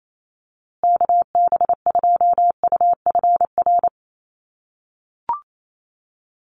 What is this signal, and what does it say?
Morse code, heard as a keyed tone, about 28 wpm, 700 Hz. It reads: KB2UFR
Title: USA Call Signs Repeated 3x 28wpm